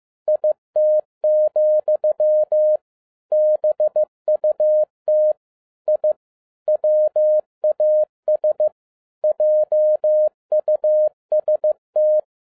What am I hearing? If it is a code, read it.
Morse code, heard as a keyed tone, about 15 wpm, 610 Hz. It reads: IT, BUT I WAS JUST